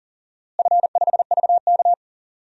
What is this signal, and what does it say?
Morse code, heard as a keyed tone, about 40 wpm, 700 Hz. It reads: F5VK
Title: DXCC-100 Call Signs - 4 Characters - Repeated 2x - 40wpm